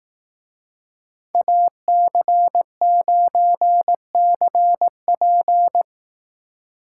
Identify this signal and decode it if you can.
Morse code, heard as a keyed tone, about 18 wpm, 700 Hz. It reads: AC9CP
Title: Speed Racing - USA Call Signs 15wpm